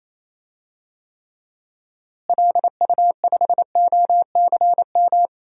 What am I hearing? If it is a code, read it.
Morse code, heard as a keyed tone, about 28 wpm, 700 Hz. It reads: LU5OCM
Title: DXCC-100 Call Signs - 6 Characters - Repeated 2x - 28wpm